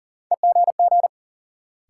Morse code, heard as a keyed tone, about 40 wpm, 700 Hz. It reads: EGG
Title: Sentences from Top 1000 Words - Review Part 7 - New Word Drill - 40wpm